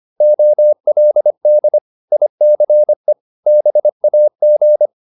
Morse code, heard as a keyed tone, about 25 wpm, 600 Hz. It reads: OLD ICE BAG